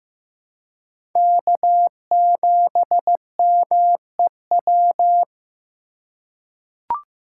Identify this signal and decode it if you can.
Morse code, heard as a keyed tone, about 15 wpm, 700 Hz. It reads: K7MEW
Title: USA Call Signs - 2x Character Spacing - 15wpm